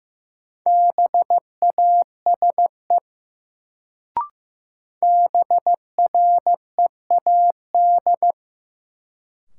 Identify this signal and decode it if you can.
Morse code, heard as a keyed tone, about 16 wpm, 700 Hz. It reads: BASE BREAD
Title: Sentences from Top 1600 Words - New Word Drill Alt - 15wpm